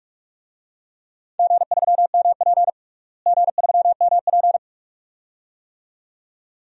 Morse code, heard as a keyed tone, about 44 wpm, 700 Hz. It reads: G3MP G3MP
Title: DXCC-100 Call Signs - 4 Characters - Repeated 2x - 45wpm